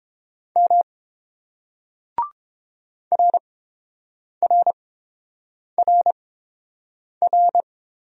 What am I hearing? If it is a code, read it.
Morse code, heard as a keyed tone, about 30 wpm, spaced as for 7 wpm, 690 Hz. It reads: M RRRR